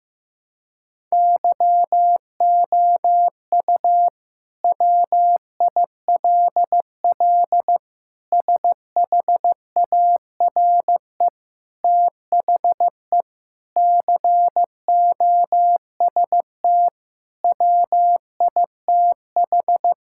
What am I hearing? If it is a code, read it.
Morse code, heard as a keyed tone, about 15 wpm, 700 Hz. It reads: YOU WILL SHARE THE COST WITH
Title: Sentences from Top 1000 Words - Review Part 6 - 15wpm